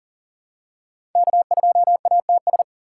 Morse code, heard as a keyed tone, about 40 wpm, 700 Hz. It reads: K2ATS
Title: USA Call Signs - 5 Characters - 3x Character Spacing - 40wpm